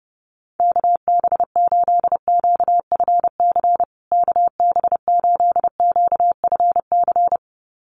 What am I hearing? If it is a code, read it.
Morse code, heard as a keyed tone, about 30 wpm, 700 Hz. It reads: KB8QFC KB8QFC
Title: USA Call Signs Repeated 2x 30wpm